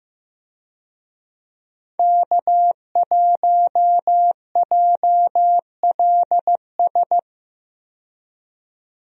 Morse code, heard as a keyed tone, about 15 wpm, 700 Hz. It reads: K1JLS
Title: USA Call Signs - 5 Characters - 15wpm